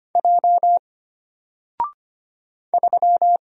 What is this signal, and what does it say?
Morse code, heard as a keyed tone, about 25 wpm, 700 Hz. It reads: J 3
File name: Single-Letter-Number-25wpm